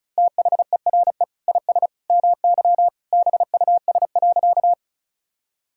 Morse code, heard as a keyed tone, about 34 wpm, 700 Hz. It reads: THERE IS MY BUS.
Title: Sentences from Top 1000 Words - Review Part 6 - 35wpm